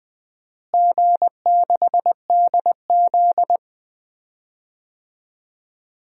Morse code, heard as a keyed tone, about 20 wpm, 700 Hz. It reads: G6DZ
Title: DXCC-100 Call Signs - 4 Characters - Repeated 3x - 20wpm